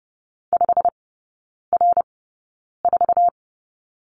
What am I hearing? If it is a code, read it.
Morse code, heard as a keyed tone, about 30 wpm, spaced as for 9 wpm, 700 Hz. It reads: 5R4
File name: course / 30:10wpm / 087 Lesson 15e - Number 5 - Three Characters 30wpm